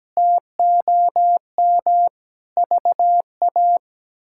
Morse code, heard as a keyed tone, about 17 wpm, 700 Hz. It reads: TOM VA
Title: K1USN SST QSO Practice - 17wpm